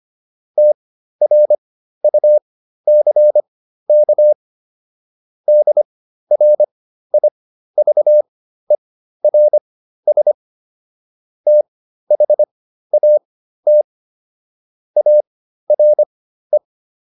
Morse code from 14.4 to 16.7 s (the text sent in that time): ARE